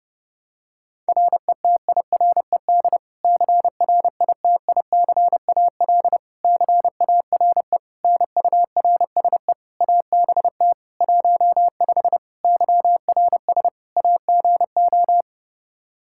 Morse code, heard as a keyed tone, about 30 wpm, 700 Hz. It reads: RETIRED CRITICAL CARE NURSE ABT 15 YRS AGO
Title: Practice QSO - Exchange 3 - More Info - 30wpm